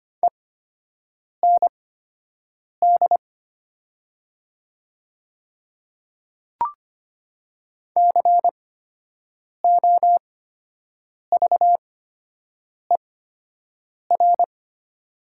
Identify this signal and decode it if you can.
Morse code, heard as a keyed tone, about 25 wpm, spaced as for 7 wpm, 700 Hz. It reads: END COVER
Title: 8x Character Spacing - Top 500 Words - 25wpm